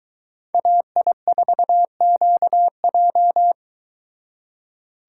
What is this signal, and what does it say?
Morse code, heard as a keyed tone, about 22 wpm, 700 Hz. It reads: AI4QJ